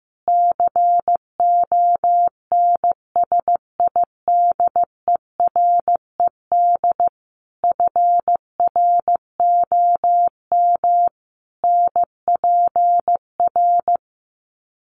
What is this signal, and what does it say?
Morse code, heard as a keyed tone, about 15 wpm, 700 Hz. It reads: CONSIDERED FROM NPR
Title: Sets of 5 Words Encore 15wpm